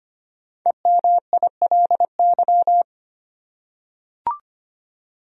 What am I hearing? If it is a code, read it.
Morse code, heard as a keyed tone, about 25 wpm, 700 Hz. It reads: EMILY
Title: USA Common Names - 25wpm